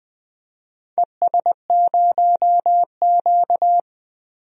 Morse code, heard as a keyed tone, about 21 wpm, 700 Hz. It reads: ES0Q
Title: DXCC-100 Call Signs - 4 Characters - 20wpm